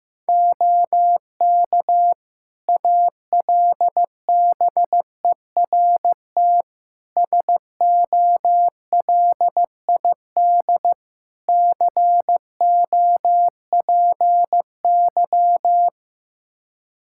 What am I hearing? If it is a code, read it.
Morse code, heard as a keyed tone, about 15 wpm, 700 Hz. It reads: OK ALBERT SOLID COPY
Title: Practice QSO - Exchange 3 - More Info - 15wpm